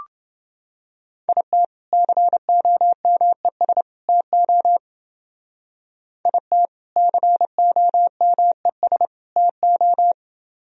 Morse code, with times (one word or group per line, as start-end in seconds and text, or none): 1.3-1.6 IT
1.9-3.8 COMES
4.1-4.8 TO
6.3-6.7 IT
7.0-9.1 COMES
9.4-10.1 TO